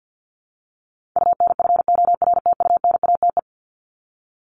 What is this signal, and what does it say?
Morse code, heard as a keyed tone, about 48 wpm, 700 Hz. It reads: UNFORTUNATE
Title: LY Suffix - Top 100 Words with Base Words 50wpm